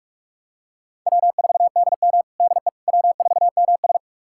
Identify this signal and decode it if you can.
Morse code, heard as a keyed tone, about 44 wpm, 700 Hz. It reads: W4DM DE W4MS
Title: USA Callsign DE Callsign - 4 Characters - 45wpm